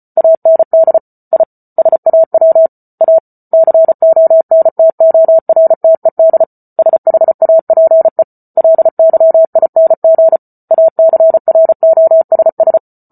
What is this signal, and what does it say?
Morse code, heard as a keyed tone, about 34 wpm, 650 Hz. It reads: AND I SAW A CONTORTED SHAPE LYING ACROSS